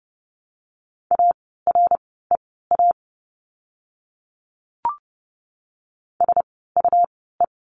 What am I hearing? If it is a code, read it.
Morse code, heard as a keyed tone, about 30 wpm, spaced as for 17 wpm, 700 Hz. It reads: AREA SUE
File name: Lesson 17f - Letter u - Words 30wpm